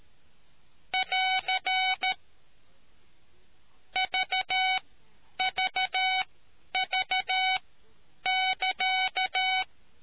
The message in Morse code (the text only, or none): <AR> VVV<KA>